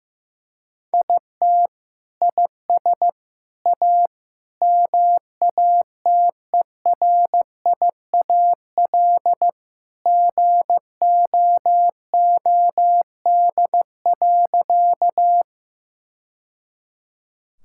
Morse code, heard as a keyed tone, about 16 wpm, 700 Hz. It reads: IT IS A MATERIAL GOOD.
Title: Sentences from Top 1000 Words - Review Part 5 - 15wpm